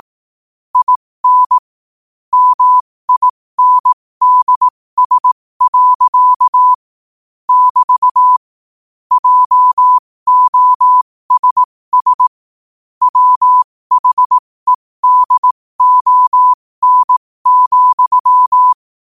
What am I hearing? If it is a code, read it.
Morse code, heard as a keyed tone, about 18 wpm, spaced as for 15 wpm, 1000 Hz. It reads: IN MINDS. <BT> JOSS WHEDON,
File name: qotdfiles / 15wpm